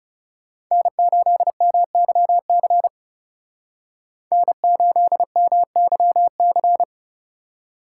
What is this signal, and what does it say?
Morse code, heard as a keyed tone, about 35 wpm, 700 Hz. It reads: N8MYC N8MYC
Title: Speed Racing - USA Call Signs 30wpm